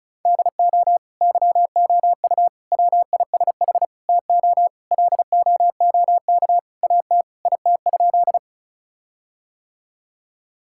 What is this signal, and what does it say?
Morse code, heard as a keyed tone, about 34 wpm, 700 Hz. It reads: DO YOU WISH TO LOOK AT IT?
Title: Sentences from Top 1600 Words 35wpm